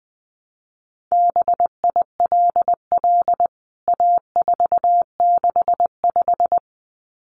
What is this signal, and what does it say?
Morse code, heard as a keyed tone, about 20 wpm, 700 Hz. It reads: BILL A465
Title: CWT Exchange 20wpm